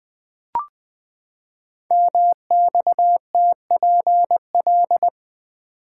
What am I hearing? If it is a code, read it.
Morse code, heard as a keyed tone, about 20 wpm, 700 Hz. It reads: MXTPL
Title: Code Groups - 5 Letters - 20wpm